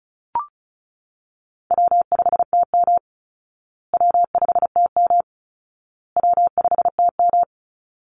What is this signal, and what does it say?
Morse code, heard as a keyed tone, about 34 wpm, 700 Hz. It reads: W5TM W5TM W5TM
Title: Call Signs Repeated 2x - 35wpm